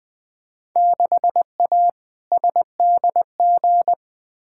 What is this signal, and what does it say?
Morse code, heard as a keyed tone, about 20 wpm, 700 Hz. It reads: 6A SDG
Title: Field Day Exchanges Encore - 20wpm